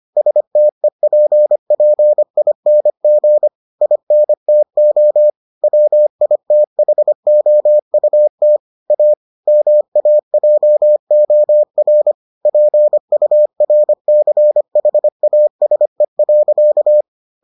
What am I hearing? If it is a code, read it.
Morse code, heard as a keyed tone, about 25 wpm, 600 Hz. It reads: STEPPING INTO WITHOUT A MAJOR PURCHASE.